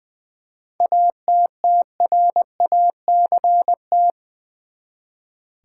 Morse code, heard as a keyed tone, about 20 wpm, 700 Hz. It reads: ATTRACT
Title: Seven Letter Words - 20wpm